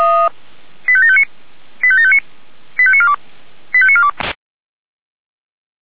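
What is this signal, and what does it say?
Five-tone selective call (ZVEI), heard as quick runs of five tones, tones of about 70 ms. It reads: 85759 85759 85942 85942
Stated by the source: Five-tone ZVEI selective call received on an SDR.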